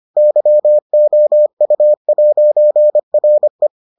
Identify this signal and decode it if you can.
Morse code, heard as a keyed tone, about 25 wpm, 600 Hz. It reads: YOU'RE